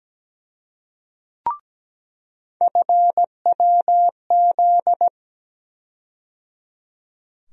Morse code, heard as a keyed tone, about 17 wpm, 700 Hz. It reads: FWZ
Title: USA Call Sign Postfixes - 3 Letters 17wpm